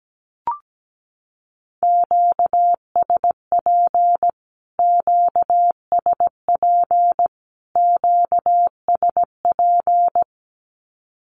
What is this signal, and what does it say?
Morse code, heard as a keyed tone, about 17 wpm, 700 Hz. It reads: QSP QSP QSP
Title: Q-Codes Repeated 3x 17wpm